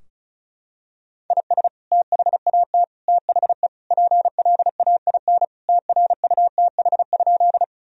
Morse code, heard as a keyed tone, about 34 wpm, 700 Hz. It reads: IS THAT THE PLAIN TRUTH?
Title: Sentences from Top 1000 Words - Review Part 2 - 35wpm